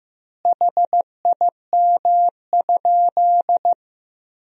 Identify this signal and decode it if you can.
Morse code, heard as a keyed tone, about 15 wpm, 700 Hz. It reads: HIM?
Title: 3x Word-Spacing - Sentences from Top 100 Words - 15wpm